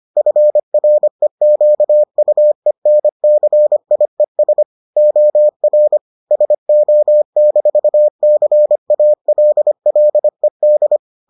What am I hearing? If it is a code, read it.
Morse code, heard as a keyed tone, about 25 wpm, 600 Hz. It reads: FREQUENCIES OR SO-CALLED